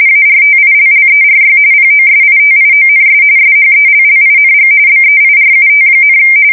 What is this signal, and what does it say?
RTTY, about 45.45 baud. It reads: ARTILLERY POSITIONS SHELLING BEIRUT,
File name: fsk170 / USS-New-Jersey